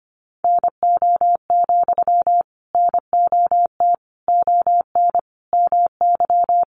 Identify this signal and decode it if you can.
Morse code, heard as a keyed tone, about 25 wpm, 700 Hz. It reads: NO, NOT ON MY
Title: Sentences from Top 900 Words 25wpm